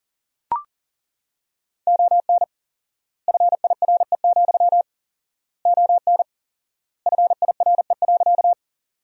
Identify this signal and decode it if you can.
Morse code, heard as a keyed tone, about 40 wpm, 700 Hz. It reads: ON FIRE, ON FIRE.
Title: 4x Word-Spacing - Sentences from Top 400 Words - 40wpm